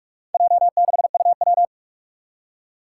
Morse code, heard as a keyed tone, about 46 wpm, 700 Hz. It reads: J6UW
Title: Four Challenging Letter-Number Combinations 45wpm